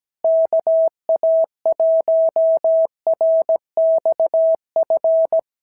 Morse code, heard as a keyed tone, about 17 wpm, 650 Hz. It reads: KA1RXF
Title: SOTA QSO Practice 17wpm